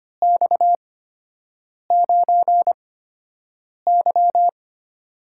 Morse code, heard as a keyed tone, about 25 wpm, spaced as for 7 wpm, 700 Hz. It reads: X9Y
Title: USA Call Signs - 4 Characters - 8x Character Spacing - 25wpm